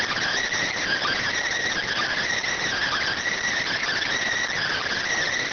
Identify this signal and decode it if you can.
Emergency Alert System (SAME) tones (header bursts), heard as harsh buzzing bursts. It is heard from 0.6 to 5.1 s.